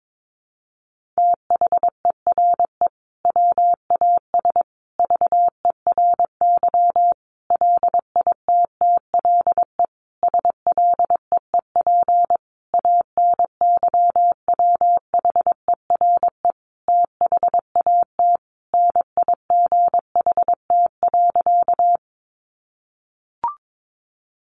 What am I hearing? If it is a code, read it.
Morse code, heard as a keyed tone, about 22 wpm, 700 Hz. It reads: THERE WAS VERY LITTLE SLEEP ANYWHERE THAT NIGHT.
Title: Sentences from Top 1400 Words 22wpm